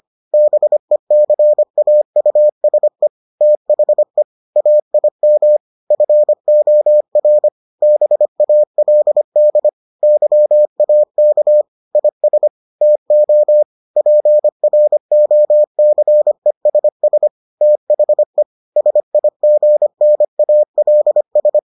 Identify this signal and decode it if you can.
Morse code, heard as a keyed tone, about 25 wpm, 600 Hz. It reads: BECAUSE THE AIM FOR BALD YAK IS TO PROCESS THE SIGNALS